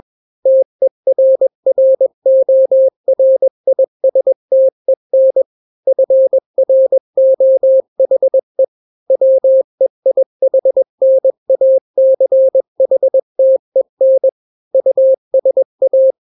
Morse code, heard as a keyed tone, about 21 wpm, 530 Hz. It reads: TERRORISTEN FROHE WEIHNACHTEN USA
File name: daily / 20wpm